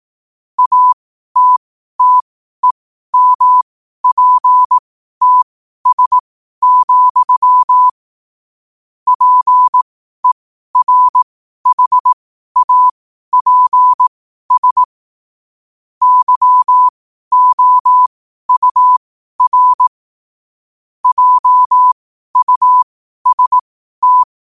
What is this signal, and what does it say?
Morse code, heard as a keyed tone, about 18 wpm, spaced as for 12 wpm, 1000 Hz. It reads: ATTEMPTS, PERHAPS YOUR JUST
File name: qotdfiles / 12wpm